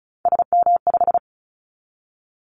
Morse code, heard as a keyed tone, about 36 wpm, 690 Hz. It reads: SM5
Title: Callsign Prefixes for Top 25 DXCC - 35wpm